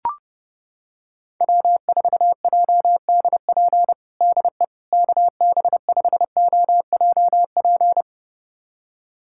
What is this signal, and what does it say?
Morse code, heard as a keyed tone, about 30 wpm, 700 Hz. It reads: W4JDP DE KB5OJP
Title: USA Callsign DE Callsign 30wpm